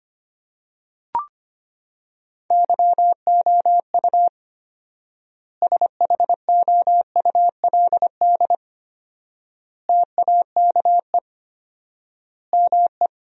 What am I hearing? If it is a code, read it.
Morse code, heard as a keyed tone, about 25 wpm, 700 Hz. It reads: YOU SHOULD TAKE ME
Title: 4x Word-Spacing - Sentences from Top 300 Words - 25wpm